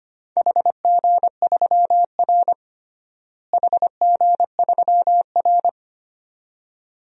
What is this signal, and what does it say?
Morse code, heard as a keyed tone, about 25 wpm, 700 Hz. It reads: HG3R HG3R
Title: DXCC-100 Call Signs - 4 Characters - Repeated 2x - 25wpm